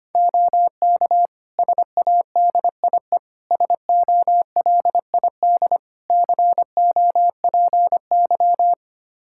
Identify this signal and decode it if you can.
Morse code, heard as a keyed tone, about 25 wpm, 700 Hz. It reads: OK SADIE SOLID COPY
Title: Practice QSO - Exchange 3 - More Info - 25wpm